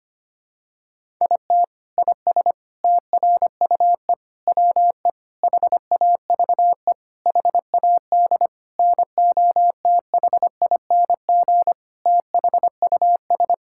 Morse code, heard as a keyed tone, about 25 wpm, 700 Hz. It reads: IT IS TRUE WE HAVE HAD NOTHING THUS
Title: Sentences from Top 1000 Words - Review Part 5 - 25wpm